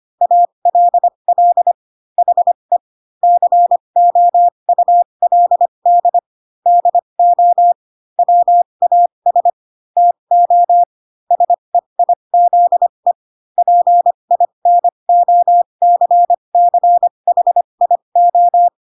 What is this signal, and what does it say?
Morse code, heard as a keyed tone, about 25 wpm, spaced as for 22 wpm, 700 Hz. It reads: ALL HE COULD DO WAS TO SEIZE PINOCCHIO
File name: Pinocchio / Pinocchio ch3 18 25 ik1hge